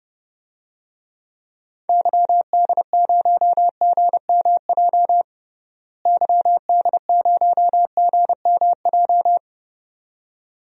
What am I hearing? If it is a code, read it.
Morse code, heard as a keyed tone, about 30 wpm, 700 Hz. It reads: YD0GMJ YD0GMJ
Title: DXCC-100 Call Signs - 6 Characters - Repeated 2x - 30wpm